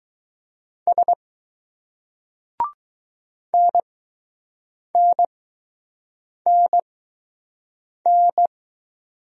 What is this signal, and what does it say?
Morse code, heard as a keyed tone, about 20 wpm, spaced as for 6 wpm, 700 Hz. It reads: S NNNN